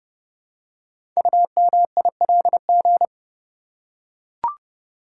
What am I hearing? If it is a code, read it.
Morse code, heard as a keyed tone, about 30 wpm, 700 Hz. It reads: UMILG
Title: Code Groups - 5 Letters - 30wpm